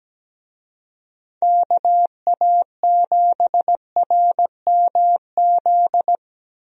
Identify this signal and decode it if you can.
Morse code, heard as a keyed tone, about 17 wpm, 700 Hz. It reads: KA7RMZ
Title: USA Call Signs - 6 Characters - Repeated 2x - 17wpm